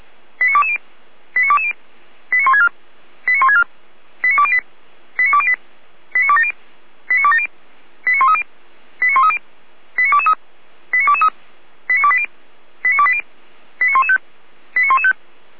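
Five-tone selective call (ZVEI), heard as quick runs of five tones, tones of about 80 ms. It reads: 89220 89220 89164 89164 89208 89208 89270 89270 89130 89130 89223 89223 89280 89280 89115 89115
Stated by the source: ZVEI five-tone call received by SDR.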